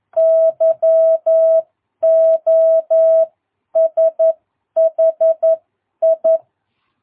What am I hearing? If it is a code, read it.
Morse code, heard as a keyed tone, about 11 wpm, spaced as for 9 wpm, 660 Hz. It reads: YOSHI
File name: jh8cbh / 2022cw